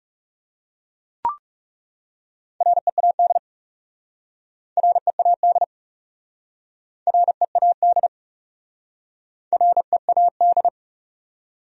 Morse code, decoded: READ READ READ READ

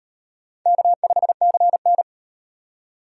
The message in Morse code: K5CN

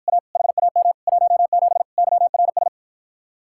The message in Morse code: A HAM 17 YRS

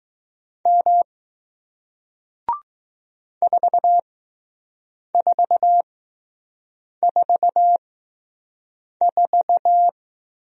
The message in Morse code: M 4444

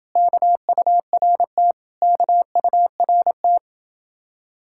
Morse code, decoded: KURT KURT

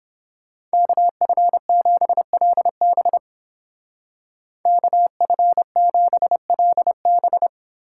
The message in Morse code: KF7LB KF7LB

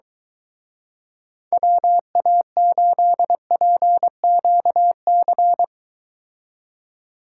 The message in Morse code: WA8PQC

